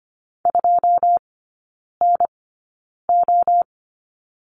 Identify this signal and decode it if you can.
Morse code, heard as a keyed tone, about 25 wpm, spaced as for 9 wpm, 700 Hz. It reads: 2NO
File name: course / 25:10wpm / 093 Lesson 16e - Letter c - Three Characters 25wpm